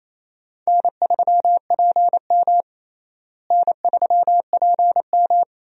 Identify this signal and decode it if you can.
Morse code, heard as a keyed tone, about 28 wpm, 700 Hz. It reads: N3PM N3PM
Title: USA Call Signs - 4 Characters - Repeated 2x - 28wpm